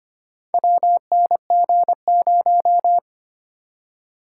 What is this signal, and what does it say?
Morse code, heard as a keyed tone, about 25 wpm, 700 Hz. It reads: WNG0
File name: practice / 25wpm / Four Letter-Number Combinations - 25wpm